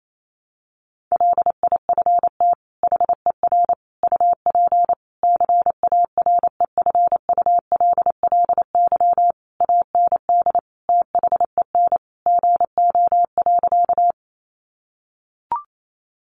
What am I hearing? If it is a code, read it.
Morse code, heard as a keyed tone, about 28 wpm, 700 Hz. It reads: LIFT HER UP CAREFULLY AND THEN GO.
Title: Sentences from Top 1300 Words 28wpm